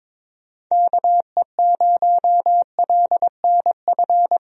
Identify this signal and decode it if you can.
Morse code, heard as a keyed tone, about 22 wpm, 700 Hz. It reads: KE0LNF
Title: USA Call Signs - 6 Characters - Repeated 3x - 22wpm